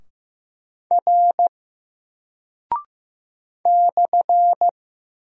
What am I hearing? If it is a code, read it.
Morse code, heard as a keyed tone, about 15 wpm, 700 Hz. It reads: R /
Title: Single Character (Letter:Number:Punctuation:Prosign) 15wpm